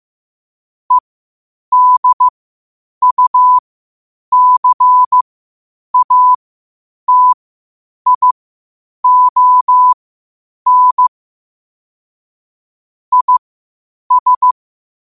Morse code, heard as a keyed tone, about 15 wpm, spaced as for 8 wpm, 1000 Hz. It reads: EDUCATION IS
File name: qotdfiles / 8wpm